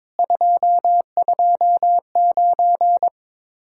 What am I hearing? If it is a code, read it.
Morse code, heard as a keyed tone, about 22 wpm, 700 Hz. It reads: 229
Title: Speed-Racing Common QSO Phrases - 22wpm